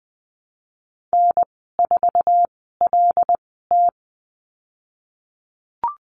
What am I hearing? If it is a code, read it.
Morse code, heard as a keyed tone, about 19 wpm, spaced as for 14 wpm, 700 Hz. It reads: N4LT
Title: 082 Lesson 14g - Number 2 - Call Signs 20wpm